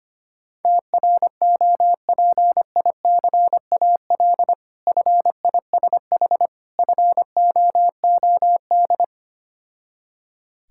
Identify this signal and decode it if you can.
Morse code, heard as a keyed tone, about 25 wpm, 700 Hz. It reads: TROPICAL FISH FOOD
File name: practice / 25wpm / Difficult Words with Hints - 25wpm